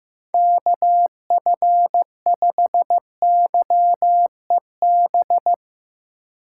Morse code, heard as a keyed tone, about 15 wpm, 700 Hz. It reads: KF5YEB
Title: USA Call Signs - 6 Characters - 3x Character Spacing - 15wpm